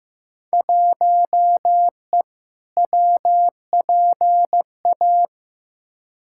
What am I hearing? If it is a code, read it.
Morse code, heard as a keyed tone, about 15 wpm, 700 Hz. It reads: 1E WPA
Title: Field Day Exchanges - 15wpm